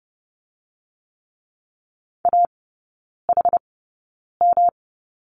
Morse code, heard as a keyed tone, about 30 wpm, spaced as for 9 wpm, 700 Hz. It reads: AHM
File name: Lesson 18e - Letter m - Three Characters 30wpm